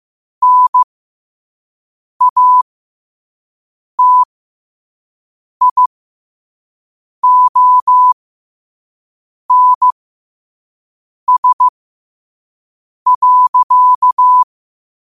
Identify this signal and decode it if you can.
Morse code, heard as a keyed tone, about 15 wpm, spaced as for 5 wpm, 1000 Hz. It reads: NATIONS.
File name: qotdfiles / 5wpm